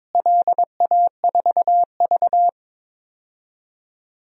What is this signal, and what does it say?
Morse code, heard as a keyed tone, about 22 wpm, 700 Hz. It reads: LA4V
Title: DXCC-100 Call Signs - 4 Characters - Repeated 2x - 22wpm